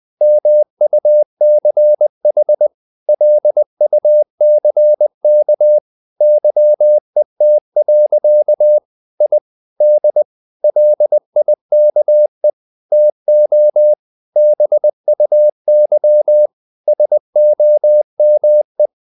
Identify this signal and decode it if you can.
Morse code, heard as a keyed tone, about 20 wpm, 600 Hz. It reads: MUCH LUCK YET. I D LIKE TO BUY SOME
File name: the old man and the sea part 2 20wpm